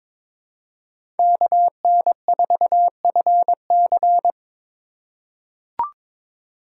Morse code, heard as a keyed tone, about 22 wpm, 700 Hz. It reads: KN4FC
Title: USA Call Signs - 5 Characters - 6x Character Spacing - 22wpm